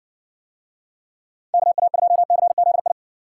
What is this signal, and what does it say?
Morse code, heard as a keyed tone, about 62 wpm, 700 Hz. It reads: KA2CZI